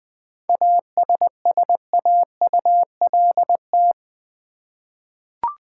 Morse code, heard as a keyed tone, about 20 wpm, 700 Hz. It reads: ASSAULT
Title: 7 Letter Words - First 3 Letters Given - 20wpm